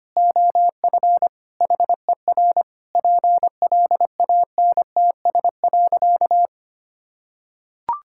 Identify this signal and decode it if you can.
Morse code, heard as a keyed tone, about 25 wpm, 700 Hz. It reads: OF HER PLANTS.
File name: practice / 25wpm / Sentences with 4-Word Phrases - 25wpm